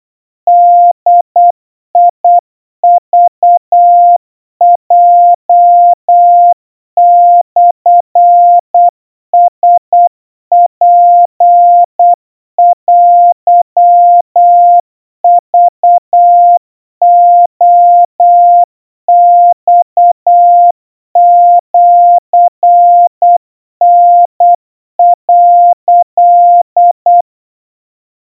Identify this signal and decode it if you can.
Morse code (Japanese Wabun code), heard as a keyed tone, about 8 wpm, 700 Hz. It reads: ボクヲモラツテクレマシタ」
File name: 30 cpm 700 Hz Crispy